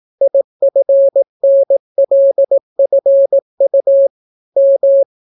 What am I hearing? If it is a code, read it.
Morse code, heard as a keyed tone, about 18 wpm, 550 Hz. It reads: IFNLFU M